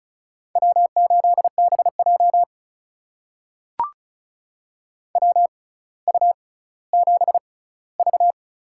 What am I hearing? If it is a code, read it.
Morse code, heard as a keyed tone, about 34 wpm, spaced as for 18 wpm, 700 Hz. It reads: W8BJ WU7V
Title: USA Call Signs - 6x Character Spacing - 35wpm